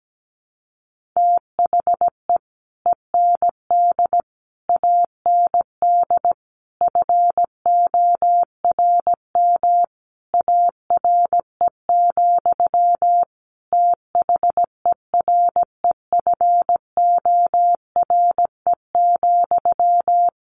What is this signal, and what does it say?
Morse code, heard as a keyed tone, about 17 wpm, 700 Hz. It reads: THE END AND FORM ARE, THEREFORE,
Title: Sentences from Top 1400 Words 17wpm